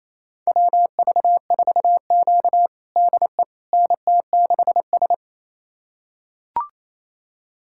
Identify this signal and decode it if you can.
Morse code, heard as a keyed tone, about 28 wpm, 700 Hz. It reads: WV4Q DE NT6S
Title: USA Callsign DE Callsign - 4 Characters - 28wpm